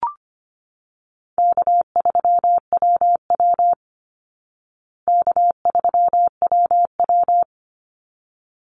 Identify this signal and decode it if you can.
Morse code, heard as a keyed tone, about 25 wpm, 700 Hz. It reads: K3WW K3WW
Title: Easy Call Signs Repeated 2x - 25wpm